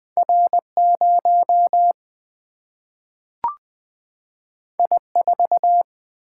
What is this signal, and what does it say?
Morse code, heard as a keyed tone, about 20 wpm, 700 Hz. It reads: R0 I4
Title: Single-Letter Single-Number 20wpm